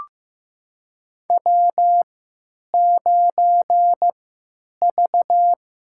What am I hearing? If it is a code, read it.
Morse code, heard as a keyed tone, about 15 wpm, spaced as for 9 wpm, 700 Hz. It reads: W9V
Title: USA Call Signs - 3x Character Spacing - 15wpm